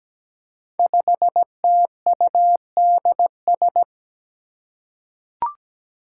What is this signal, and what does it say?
Morse code, heard as a keyed tone, about 17 wpm, 700 Hz. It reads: 5TUDS
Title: Five Challenging Letter-Number Combinations 17wpm